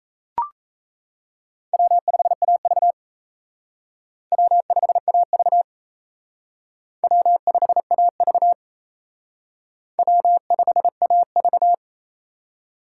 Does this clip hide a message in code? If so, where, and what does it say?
Yes: Morse code from 1.7 to 11.7 s, heard as a keyed tone, about 36 wpm, 700 Hz. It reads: W5AV W5AV W5AV W5AV